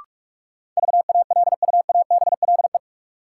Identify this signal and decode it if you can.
Morse code, heard as a keyed tone, about 46 wpm, 700 Hz. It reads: VALUABLE
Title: Sentences from Top 1100 Words - New Word Drill Alt - 45wpm